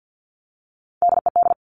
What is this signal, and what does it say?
Morse code, heard as a keyed tone, about 50 wpm, 700 Hz. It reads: DED
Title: Top Three Letter Combinations in Words 50wpm